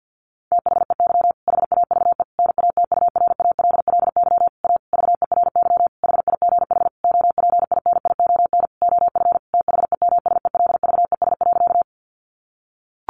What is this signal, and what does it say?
Morse code, heard as a keyed tone, about 50 wpm, 700 Hz. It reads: THEY HAVE NATURALLY A VERY HIGH OPINION OF THEMSELVES.
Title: Sentences from Top 1300 Words 50wpm